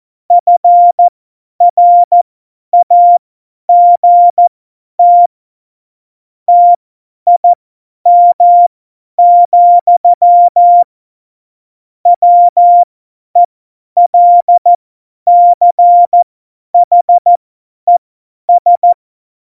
Morse code, heard as a keyed tone, about 14 wpm, spaced as for 10 wpm, 700 Hz. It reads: FRAGT TIM, WELCHES
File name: Morsekurs / Witze / Witz 26. November 2019, 50 F70